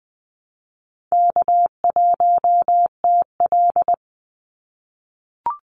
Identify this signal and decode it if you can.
Morse code, heard as a keyed tone, about 20 wpm, 700 Hz. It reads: K1TL
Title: Easy Call Signs - 20wpm